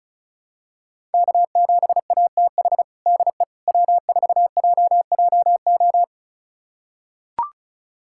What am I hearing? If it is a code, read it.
Morse code, heard as a keyed tone, about 36 wpm, 700 Hz. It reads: K7ATH DE W4JJO
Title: USA Callsign DE Callsign - 5 Characters - 35wpm